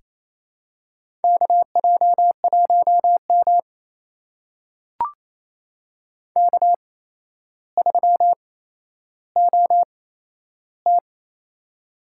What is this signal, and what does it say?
Morse code, heard as a keyed tone, about 28 wpm, spaced as for 12 wpm, 700 Hz. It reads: KJ1M K3OT
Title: USA Call Signs - 4 Characters - 8x Character Spacing - 28wpm